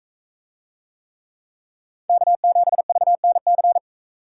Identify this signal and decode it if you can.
Morse code, heard as a keyed tone, about 41 wpm, 700 Hz. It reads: K7VNC